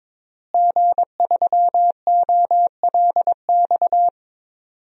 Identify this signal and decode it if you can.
Morse code, heard as a keyed tone, about 22 wpm, 700 Hz. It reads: G3OLX
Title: DXCC-100 Call Signs - 5 Characters - Repeated 3x - 22wpm